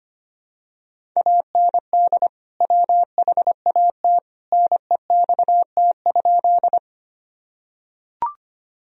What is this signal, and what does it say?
Morse code, heard as a keyed tone, about 25 wpm, 700 Hz. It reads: AND WHAT NEXT?
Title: Sentences from Top 1000 Words - Review Part 1 - 25wpm